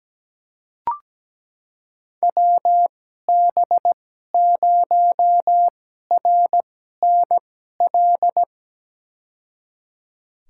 Morse code, heard as a keyed tone, about 17 wpm, spaced as for 12 wpm, 700 Hz. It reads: WB0RNL
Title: USA Call Signs - 6 Characters - 2x Character Spacing - 17wpm